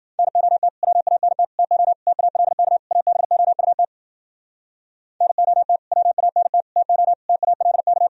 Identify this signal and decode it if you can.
Morse code, heard as a keyed tone, about 52 wpm, 700 Hz. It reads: NOT WANT TO TALK ABOUT NOT WANT TO TALK